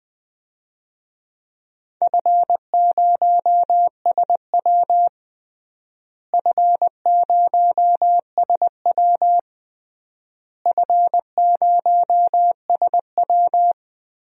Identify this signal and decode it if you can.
Morse code, heard as a keyed tone, about 20 wpm, 700 Hz. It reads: F0SW F0SW F0SW
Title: DXCC-100 Call Signs - 4 Characters - Repeated 3x - 20wpm